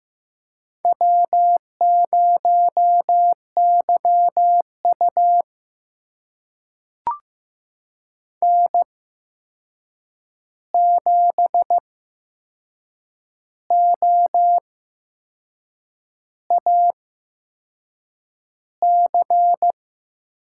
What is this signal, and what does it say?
Morse code, heard as a keyed tone, about 15 wpm, spaced as for 6 wpm, 700 Hz. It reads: W0YU N7OAC
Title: USA Call Signs - 8x Character Spacing - 15wpm